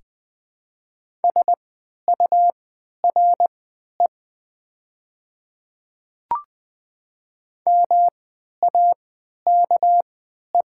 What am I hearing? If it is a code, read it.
Morse code, heard as a keyed tone, about 20 wpm, spaced as for 11 wpm, 700 Hz. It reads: SURE MAKE
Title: 3x Character Spacing - Top 300 Words - 20wpm